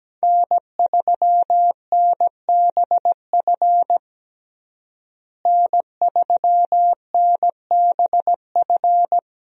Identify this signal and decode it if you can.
Morse code, heard as a keyed tone, about 17 wpm, 700 Hz. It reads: N3NBF N3NBF
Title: USA Call Signs - 5 Characters - Repeated 2x - 17wpm